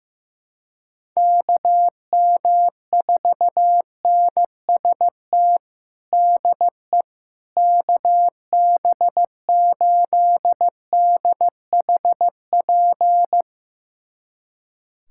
Morse code, heard as a keyed tone, about 15 wpm, 700 Hz. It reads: KM4NST DE KB8DHP
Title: USA Callsign DE Callsign 15wpm